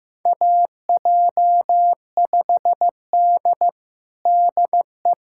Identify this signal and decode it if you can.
Morse code, heard as a keyed tone, about 15 wpm, 700 Hz. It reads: AJ5D DE
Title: USA Callsign DE Callsign - 4 Characters - 15wpm